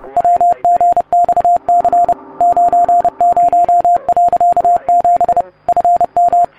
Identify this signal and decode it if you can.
Morse code, heard as a keyed tone, about 30 wpm, 700 Hz. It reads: WGXC 90.7 FM